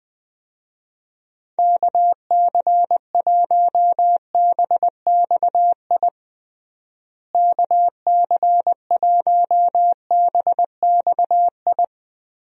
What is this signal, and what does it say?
Morse code, heard as a keyed tone, about 20 wpm, 700 Hz. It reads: KC1BXI KC1BXI
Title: USA Call Signs - 6 Characters - Repeated 2x - 20wpm